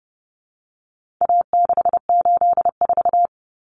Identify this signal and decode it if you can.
Morse code, heard as a keyed tone, about 30 wpm, 700 Hz. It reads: A684
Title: Common Cut Numbers - 4-Digit 30wpm